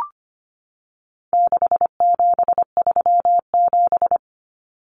Morse code, heard as a keyed tone, about 25 wpm, 700 Hz. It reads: 6737
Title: Common Cut Numbers - 4-Digit 25wpm